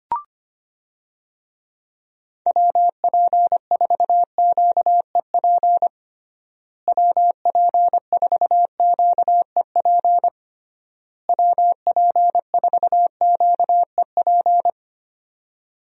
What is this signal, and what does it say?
Morse code, heard as a keyed tone, about 25 wpm, 700 Hz. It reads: WP4QEP WP4QEP WP4QEP
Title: USA Call Signs - 6 Characters - Repeated 3x - 25wpm